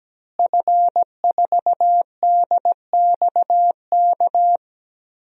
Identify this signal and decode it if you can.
Morse code, heard as a keyed tone, about 17 wpm, 700 Hz. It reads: F4DXK
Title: DXCC-100 Call Signs - 5 Characters - Repeated 2x - 17wpm